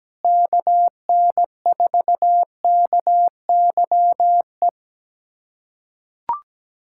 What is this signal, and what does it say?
Morse code, heard as a keyed tone, about 17 wpm, 700 Hz. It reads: KN4KYE
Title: USA Callsign DE Callsign 17wpm